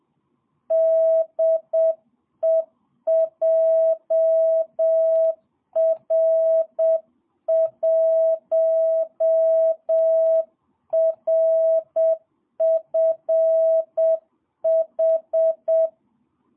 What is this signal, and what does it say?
Morse code, heard as a keyed tone, about 7 wpm, 650 Hz. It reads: DE JR1RFH
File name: jh8cbh / 2022cw